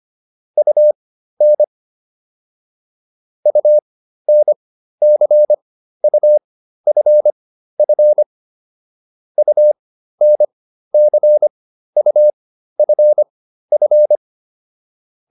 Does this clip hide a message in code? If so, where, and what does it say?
Yes: Morse code from 0.6 to 14.2 s, heard as a keyed tone, about 25 wpm, spaced as for 13 wpm, 600 Hz. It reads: UN UNCUFF UNCUFF